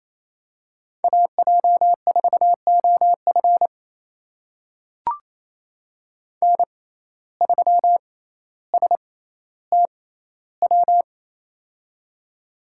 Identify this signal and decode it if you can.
Morse code, heard as a keyed tone, about 28 wpm, spaced as for 14 wpm, 700 Hz. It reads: AJ4OF N3STW
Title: USA Call Signs - 5 Characters - 6x Character Spacing - 28wpm